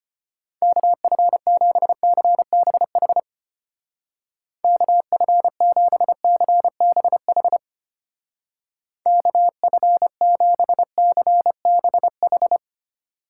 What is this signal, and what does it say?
Morse code, heard as a keyed tone, about 29 wpm, 700 Hz. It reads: KF7CBH KF7CBH KF7CBH
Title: Speed Racing - USA Call Signs 25wpm